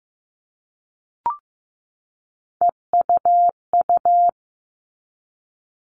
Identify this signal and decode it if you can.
Morse code, heard as a keyed tone, about 15 wpm, 700 Hz. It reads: EUU
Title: All Cut Numbers - 3-Digit 15wpm